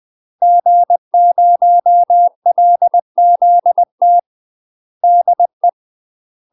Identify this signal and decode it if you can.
Morse code, heard as a keyed tone, about 20 wpm, 700 Hz. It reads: G0LZT DE
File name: UK MASTER QSOs TRAINING/20WPM